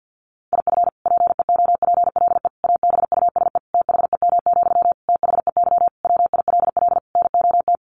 Morse code, heard as a keyed tone, about 50 wpm, 700 Hz. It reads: IF PEOPLE ABUSE THEM, THEY WILL NOT
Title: Sentences from Top 1300 Words 50wpm